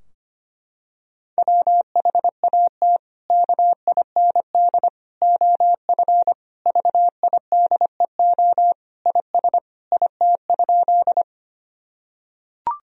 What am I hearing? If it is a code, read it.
Morse code, heard as a keyed tone, about 25 wpm, 700 Hz. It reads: WHAT KIND OF VIDEO IS IT?
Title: Sentences from Top 1000 Words - Review Part 4 - 25wpm